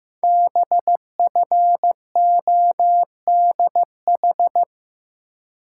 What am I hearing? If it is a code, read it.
Morse code, heard as a keyed tone, about 15 wpm, 700 Hz. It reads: BFODH
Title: Code Groups - 5 Letters - 15wpm